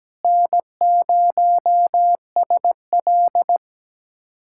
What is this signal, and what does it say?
Morse code, heard as a keyed tone, about 17 wpm, 700 Hz. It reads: N0SL
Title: USA Callsign DE Callsign - 4 Characters - 17wpm